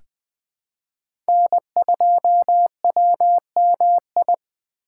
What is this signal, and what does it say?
Morse code, heard as a keyed tone, about 20 wpm, 700 Hz. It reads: N2WMI
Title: USA Call Signs - 5 Characters - 6x Character Spacing - 20wpm